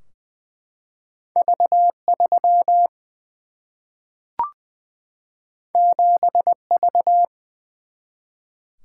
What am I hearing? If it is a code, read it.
Morse code, heard as a keyed tone, about 20 wpm, 700 Hz. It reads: V 3 7 V
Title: Single-Letter Single-Number 20wpm